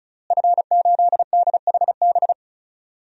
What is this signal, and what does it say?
Morse code, heard as a keyed tone, about 34 wpm, 690 Hz. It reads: F8DHB
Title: DXCC-100 Call Signs - 5 Characters - Repeated 3x - 35wpm